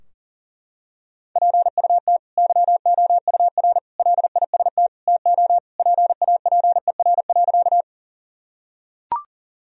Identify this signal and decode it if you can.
Morse code, heard as a keyed tone, about 40 wpm, 700 Hz. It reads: PUT YOUR LIST TO PAPER.
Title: Sentences from Top 1000 Words - Review Part 2 - 40wpm